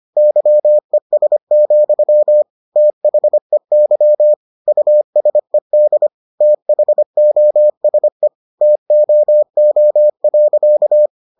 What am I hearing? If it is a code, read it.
Morse code, heard as a keyed tone, about 25 wpm, 600 Hz. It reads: YES, THEY USED THOSE TOO.